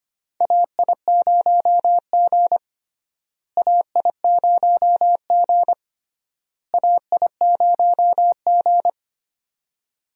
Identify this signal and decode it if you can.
Morse code, heard as a keyed tone, about 25 wpm, 700 Hz. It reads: AI0G AI0G AI0G
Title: USA Call Signs - 4 Characters - Repeated 3x - 25wpm